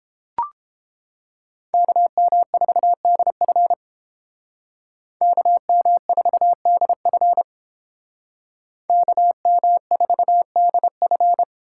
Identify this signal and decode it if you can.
Morse code, heard as a keyed tone, about 29 wpm, 700 Hz. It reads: KM4DF KM4DF KM4DF